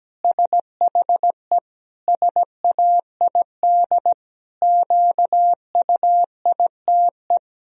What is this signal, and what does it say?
Morse code, heard as a keyed tone, about 17 wpm, 700 Hz. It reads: SHE SAID QUITE
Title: Sentences from Top 1000 Words - Review Part 4 - 17wpm